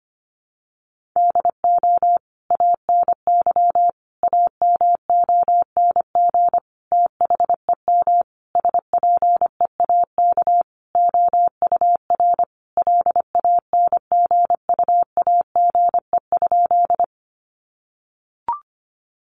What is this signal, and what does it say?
Morse code, heard as a keyed tone, about 25 wpm, 700 Hz. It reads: DO ANY AMONG THEM SPEAK OUR LANGUAGE?
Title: Sentences from Top 1100 Words 25wpm